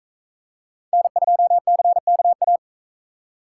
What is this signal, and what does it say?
Morse code, heard as a keyed tone, about 44 wpm, 700 Hz. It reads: N2CKA